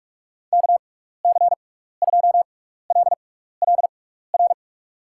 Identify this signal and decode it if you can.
Morse code, heard as a keyed tone, about 46 wpm, spaced as for 15 wpm, 700 Hz. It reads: KC2LLR
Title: USA Call Signs - 6x Character Spacing - 45wpm